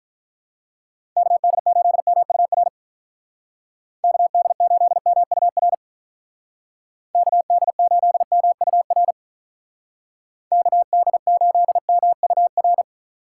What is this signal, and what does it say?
Morse code, heard as a keyed tone, about 47 wpm, 700 Hz. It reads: KD8MUR KD8MUR KD8MUR KD8MUR